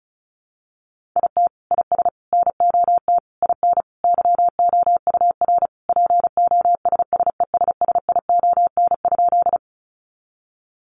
Morse code, heard as a keyed tone, about 34 wpm, 700 Hz. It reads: IT IS NOT IN YOUR POSSESSION?
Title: Sentences from Top 1400 Words 35wpm